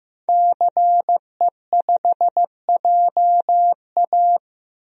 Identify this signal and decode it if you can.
Morse code, heard as a keyed tone, about 15 wpm, 700 Hz. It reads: CE5JA
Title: DXCC-100 Call Signs - 5 Characters - Repeated 3x - 15wpm